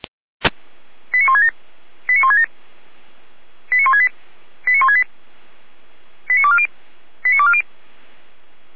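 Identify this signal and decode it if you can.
Five-tone selective call (ZVEI), heard as quick runs of five tones, tones of about 70 ms. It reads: 89167 89167 89168 89168 89240 89240
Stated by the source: ZVEI five-tone selective call 89166 received by SDR.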